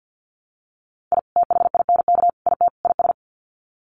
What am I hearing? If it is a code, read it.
Morse code, heard as a keyed tone, about 50 wpm, 700 Hz. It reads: I THINK IT IS
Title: Sets of 4 Words 50wpm